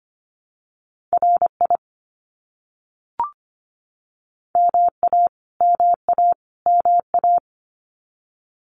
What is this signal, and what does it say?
Morse code, heard as a keyed tone, about 25 wpm, 700 Hz. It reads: RI MA MA MA
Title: US States Repeated 3x 25wpm